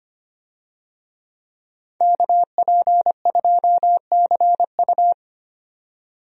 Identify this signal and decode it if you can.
Morse code, heard as a keyed tone, about 25 wpm, 700 Hz. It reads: KP2CU
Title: DXCC-100 Call Signs - 5 Characters - 25wpm